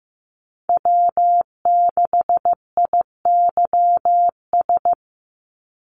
Morse code, heard as a keyed tone, about 15 wpm, 700 Hz. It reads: W6IYS
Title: Call Signs Repeated 2x - 15wpm